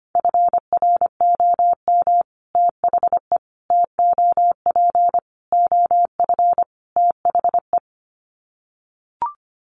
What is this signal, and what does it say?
Morse code, heard as a keyed tone, about 25 wpm, 700 Hz. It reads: FROM THE TOP OF THE
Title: Sets of 5 Words Encore 25wpm